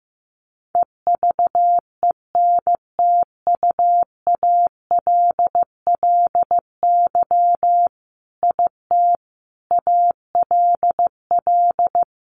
Morse code, heard as a keyed tone, about 15 wpm, 700 Hz. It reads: EVENTUALLY IT ALL
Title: A Sound Of Thunder 15wpm